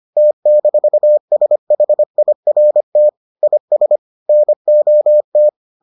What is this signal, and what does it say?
Morse code, heard as a keyed tone, about 25 wpm, 600 Hz. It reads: T-SHIRT IS NOT